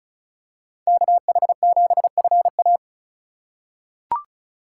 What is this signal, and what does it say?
Morse code, heard as a keyed tone, about 34 wpm, 700 Hz. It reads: KH7FA
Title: USA Call Signs - 5 Characters - 2x Character Spacing - 35wpm